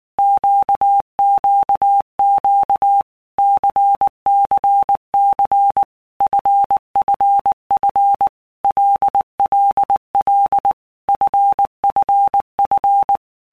Morse code, heard as a keyed tone, about 19 wpm, 800 Hz. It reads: QQQ CCC FFF LLL FFF
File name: Basic Letter Groups By Blocks / Groups@18 / LFCYQ@18